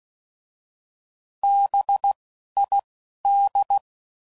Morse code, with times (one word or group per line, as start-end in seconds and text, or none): 1.4-2.1 B
2.6-2.8 I
3.3-3.8 D